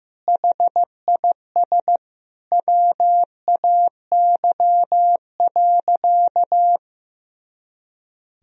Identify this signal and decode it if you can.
Morse code, heard as a keyed tone, about 15 wpm, 700 Hz. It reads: HIS WAY.
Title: Sentences from Top 300 Words and Abbreviations 15wpm